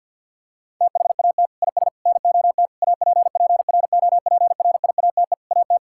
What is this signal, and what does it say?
Morse code, heard as a keyed tone, about 53 wpm, 690 Hz. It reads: THAT IS NOT APPROPRIATE AT
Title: Sentences from Top 1000 Words - Review Part 5 - 50wpm